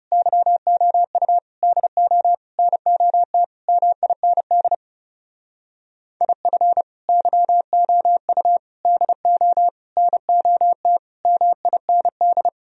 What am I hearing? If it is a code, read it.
Morse code, heard as a keyed tone, about 36 wpm, 700 Hz. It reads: YOU DO NOT MIND IF YOU DO NOT MIND
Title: Speed-Racing - Sets of 5 Words 30wpm